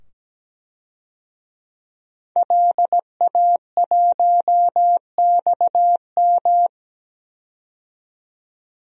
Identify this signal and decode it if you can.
Morse code, heard as a keyed tone, about 17 wpm, 700 Hz. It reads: LA1XM
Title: DXCC-100 Call Signs - 5 Characters - 17wpm